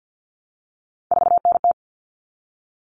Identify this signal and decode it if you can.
Morse code, heard as a keyed tone, about 50 wpm, 700 Hz. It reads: 4NT
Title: Common Cut Numbers - 3-Digit 50wpm